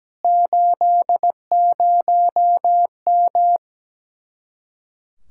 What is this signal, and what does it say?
Morse code, heard as a keyed tone, about 17 wpm, 700 Hz. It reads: 80M
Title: Speed-Racing Common QSO Phrases - 17wpm